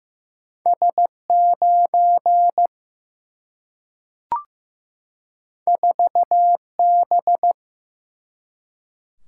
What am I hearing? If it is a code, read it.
Morse code, heard as a keyed tone, about 15 wpm, 700 Hz. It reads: S 9 4 B
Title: Single-Letter Single-Number 15wpm